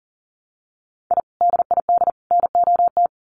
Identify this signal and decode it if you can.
Morse code, heard as a keyed tone, about 40 wpm, 700 Hz. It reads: I DID NOT
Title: Sets of 3 Words 40wpm